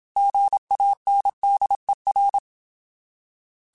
Morse code, heard as a keyed tone, about 27 wpm, 800 Hz. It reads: GANDER
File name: Alphabetical / Alphabetical mp3s / Gs @25